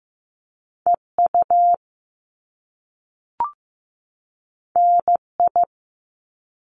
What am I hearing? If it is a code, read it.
Morse code, heard as a keyed tone, about 15 wpm, 700 Hz. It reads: EU NI
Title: Top Two Letter Combinations in Words 15wpm